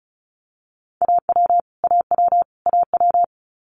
Morse code, heard as a keyed tone, about 36 wpm, 700 Hz. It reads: AW AW AW
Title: Multi-letter Phonemes Repeated 3x 35wpm